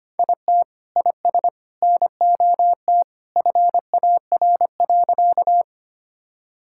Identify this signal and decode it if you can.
Morse code, heard as a keyed tone, about 25 wpm, 700 Hz. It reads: IT IS NOT FAR.
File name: practice / 25wpm / Sentences with 2-Word Phrases - 25wpm